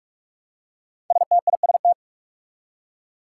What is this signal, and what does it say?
Morse code, heard as a keyed tone, about 44 wpm, 700 Hz. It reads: STIST
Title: Five Challenging Letter-Number Combinations 45wpm